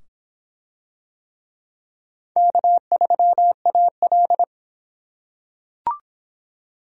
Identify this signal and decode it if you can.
Morse code, heard as a keyed tone, about 22 wpm, 700 Hz. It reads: K3AL